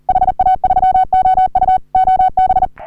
Morse code, heard as a keyed tone, about 40 wpm, 750 Hz. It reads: HA3OU OB
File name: CW / HA3OU RHOBCW REC